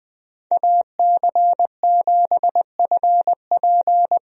Morse code, heard as a keyed tone, about 20 wpm, 700 Hz. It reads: AC7FP
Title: USA Call Signs - 5 Characters - 20wpm